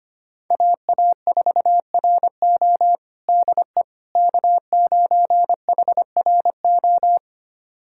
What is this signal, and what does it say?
Morse code, heard as a keyed tone, about 25 wpm, 700 Hz. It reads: AA4RO DE K9HRO
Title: USA Callsign DE Callsign - 5 Characters - 25wpm